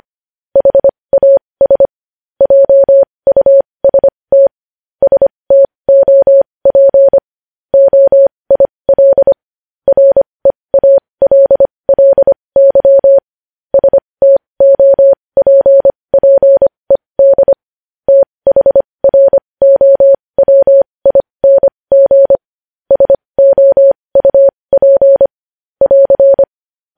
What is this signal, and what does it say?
Morse code, heard as a keyed tone, about 25 wpm, spaced as for 20 wpm, 550 Hz. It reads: HAS JUST STOP OIL REALLY STOPPED THROWING SOUP <AR>
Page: news-headlines-in-morse-code-at-20-wpm